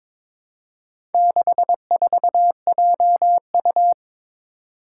Morse code, heard as a keyed tone, about 22 wpm, 700 Hz. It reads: 64JU
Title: Four Challenging Letter-Number Combinations 22wpm